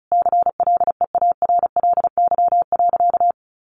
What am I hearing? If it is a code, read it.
Morse code, heard as a keyed tone, about 34 wpm, 700 Hz. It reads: CLEARLY.
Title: Sentences from Top 1300 Words 35wpm